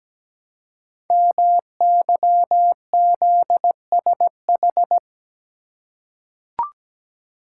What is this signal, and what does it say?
Morse code, heard as a keyed tone, about 17 wpm, 700 Hz. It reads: MYZSH
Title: Code Groups - 5 Letter:Numbers - 17wpm